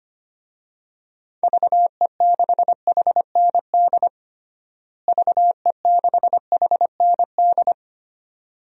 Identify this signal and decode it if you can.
Morse code, heard as a keyed tone, about 25 wpm, 700 Hz. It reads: VE6HND VE6HND
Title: DXCC-100 Call Signs - 6 Characters - Repeated 2x - 25wpm